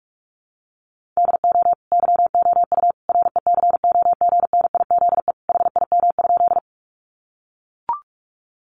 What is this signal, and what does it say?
Morse code, heard as a keyed tone, about 44 wpm, 700 Hz. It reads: DO YOU RECOGNIZE HIM?
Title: Sentences from Top 1400 Words 45wpm